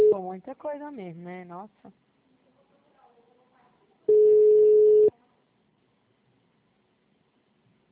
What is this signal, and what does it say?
Ringback tone (Europe, 425 Hz), with speech.